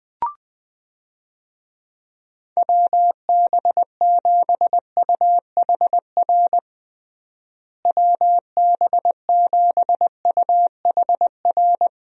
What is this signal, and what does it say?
Morse code, heard as a keyed tone, about 21 wpm, 700 Hz. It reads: WB7UHR WB7UHR
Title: USA Call Signs - 6 Characters - Repeated 3x - 20wpm